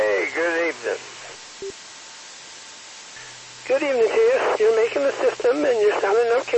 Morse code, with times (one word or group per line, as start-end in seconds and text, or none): none